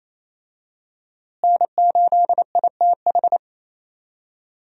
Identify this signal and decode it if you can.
Morse code, heard as a keyed tone, about 28 wpm, 700 Hz. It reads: N8ITH
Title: USA Call Signs - 5 Characters - Repeated 3x - 28wpm